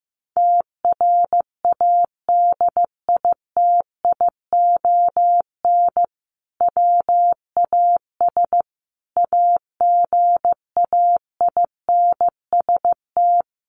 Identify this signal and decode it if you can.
Morse code, heard as a keyed tone, about 15 wpm, 700 Hz. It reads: TRADITION WAS AGAINST
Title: Sentences from Top 1500 Words 15wpm